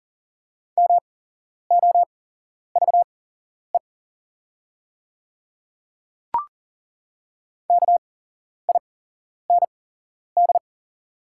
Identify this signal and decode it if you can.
Morse code, heard as a keyed tone, about 40 wpm, spaced as for 11 wpm, 700 Hz. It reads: MOVE KIND
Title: 8x Character Spacing - Top 300 Words - 40wpm